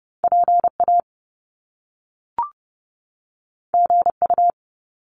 Morse code, heard as a keyed tone, about 30 wpm, 700 Hz. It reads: PA GU
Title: Top Two Letter Combinations in Words 30wpm